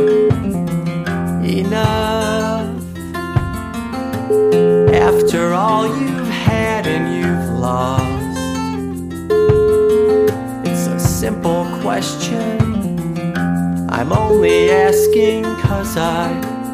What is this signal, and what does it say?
Ringback tone (Europe, 425 Hz), with speech.